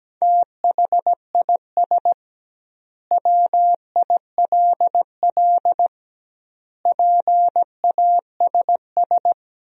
Morse code, heard as a keyed tone, about 17 wpm, 700 Hz. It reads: THIS WILL PASS
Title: 2x Word-Spacing - Sentences from Top 400 Words - 17wpm